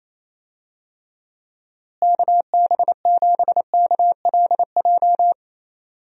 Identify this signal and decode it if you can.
Morse code, heard as a keyed tone, about 27 wpm, 700 Hz. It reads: KB7KLJ
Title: USA Call Signs - 6 Characters - Repeated 2x - 28wpm